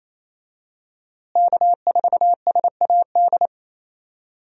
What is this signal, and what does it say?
Morse code, heard as a keyed tone, about 28 wpm, 700 Hz. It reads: K4SAD
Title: USA Call Signs - 5 Characters - Repeated 3x - 28wpm